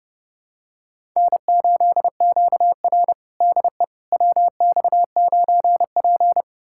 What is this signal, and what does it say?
Morse code, heard as a keyed tone, about 30 wpm, 700 Hz. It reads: N8QR DE WX9P
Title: USA Callsign DE Callsign - 4 Characters - 30wpm